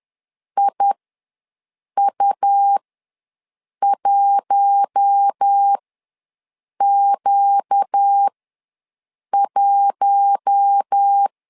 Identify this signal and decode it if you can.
Morse code, heard as a keyed tone, about 10 wpm, spaced as for 6 wpm, 800 Hz. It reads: IU1Q1
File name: LEZIONI / 4 Q1 ws 10wpm 2538